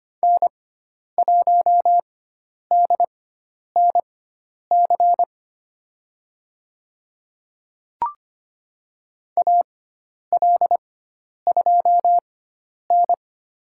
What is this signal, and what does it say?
Morse code, heard as a keyed tone, about 25 wpm, spaced as for 10 wpm, 700 Hz. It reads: N1DNC AL2N
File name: Lesson 16g - Letter c - Call Signs 25wpm